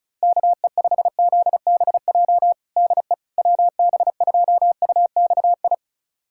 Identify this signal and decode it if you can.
Morse code, heard as a keyed tone, about 34 wpm, 700 Hz. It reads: KE5ZBJ DE WB2UXI
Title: USA Callsign DE Callsign 35wpm